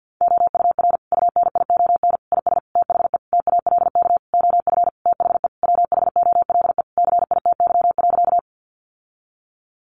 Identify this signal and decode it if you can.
Morse code, heard as a keyed tone, about 50 wpm, 700 Hz. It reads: OUR UNION IS THE TALK OF THE WHOLE CITY.
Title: Sentences from Top 1400 Words 50wpm